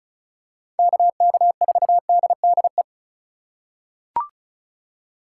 Morse code, heard as a keyed tone, about 36 wpm, 700 Hz. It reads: KK4DDE
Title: USA Call Signs - 6 Characters - 3x Character Spacing - 35wpm